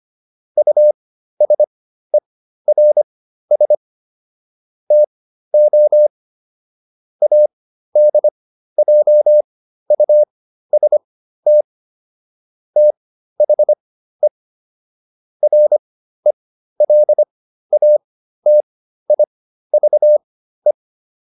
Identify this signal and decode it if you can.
Morse code, heard as a keyed tone, about 25 wpm, spaced as for 13 wpm, 600 Hz. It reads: USERS TO ADJUST THE RELATIVE